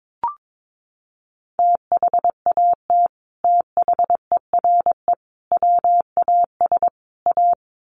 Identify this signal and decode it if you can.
Morse code, heard as a keyed tone, about 22 wpm, 700 Hz. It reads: THAT THERE WAS A
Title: Sets of 4 Words 22wpm